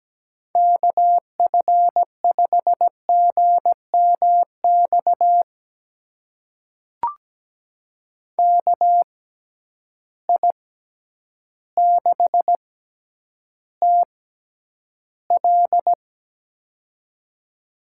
Morse code, heard as a keyed tone, about 17 wpm, spaced as for 9 wpm, 700 Hz. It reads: KF5GMX KI6TL
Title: USA Call Signs - 6x Character Spacing - 17wpm